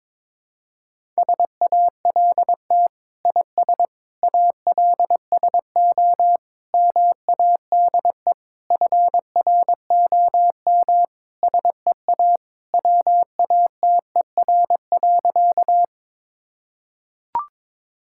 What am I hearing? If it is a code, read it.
Morse code, heard as a keyed tone, about 22 wpm, 700 Hz. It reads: SALT IS ALSO MADE FROM SEA WATER.
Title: Sentences from Top 1000 Words - Review Part 5 - 22wpm